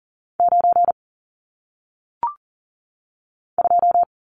Morse code, heard as a keyed tone, about 40 wpm, 700 Hz. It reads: 9 2
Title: Numbers - 1-Digit 40wpm